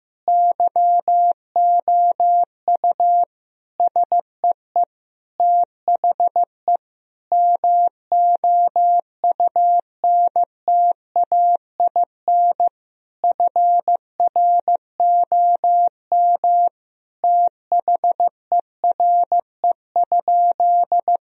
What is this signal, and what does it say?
Morse code, heard as a keyed tone, about 15 wpm, 700 Hz. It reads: YOU SEE THE MOUNTAIN FROM THERE?
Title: Sentences from Top 1000 Words - Review Part 1 - 15wpm